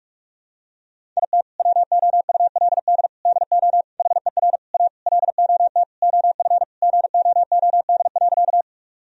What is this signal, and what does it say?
Morse code, heard as a keyed tone, about 46 wpm, 700 Hz. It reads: IT WOULD DO HER A LOT OF GOOD.
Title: Sentences from Top 1000 Words - Review Part 3 - 45wpm